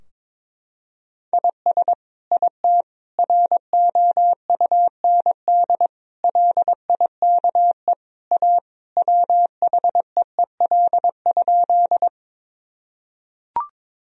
Morse code, heard as a keyed tone, about 22 wpm, 700 Hz. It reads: IS IT ROUND LIKE A WHEEL?
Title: Sentences from Top 1000 Words - Review Part 3 - 22wpm